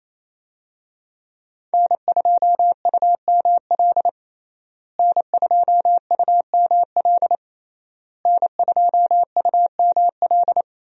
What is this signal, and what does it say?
Morse code, heard as a keyed tone, about 28 wpm, 700 Hz. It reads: N2UML N2UML N2UML
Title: USA Call Signs - 5 Characters - Repeated 3x - 28wpm